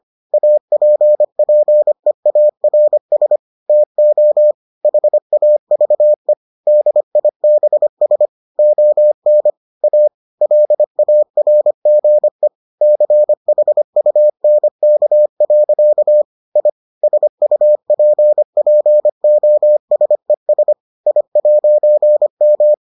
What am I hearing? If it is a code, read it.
Morse code, heard as a keyed tone, about 25 wpm, 600 Hz. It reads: APPEARS TO HAVE DIBS ON A LARGE CHUNK. I SUPPOSES I'M